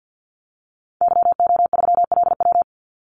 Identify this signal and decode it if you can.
Morse code, heard as a keyed tone, about 53 wpm, 700 Hz. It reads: YO3LW
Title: Call Signs Repeated 2x - 50wpm